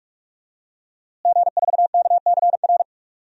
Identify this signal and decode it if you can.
Morse code, heard as a keyed tone, about 46 wpm, 700 Hz. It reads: G4KCR
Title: DXCC-100 Call Signs - 5 Characters - Repeated 2x - 45wpm